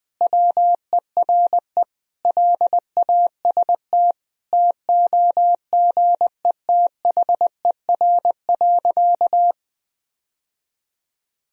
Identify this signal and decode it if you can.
Morse code, heard as a keyed tone, about 20 wpm, 700 Hz. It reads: WERE LAST TOGETHER.
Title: Sentences from Top 1000 Words - Review Part 1 - 20wpm